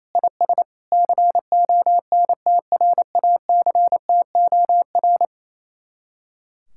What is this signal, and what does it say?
Morse code, heard as a keyed tone, about 27 wpm, 700 Hz. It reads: IS CONTRACTOR
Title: Speed-Racing Common QSO Phrases Encore - 28wpm